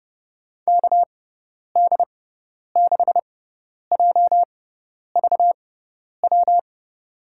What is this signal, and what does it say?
Morse code, heard as a keyed tone, about 32 wpm, spaced as for 10 wpm, 700 Hz. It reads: KD6JVW
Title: USA Call Signs - 6 Characters - 6x Character Spacing - 30wpm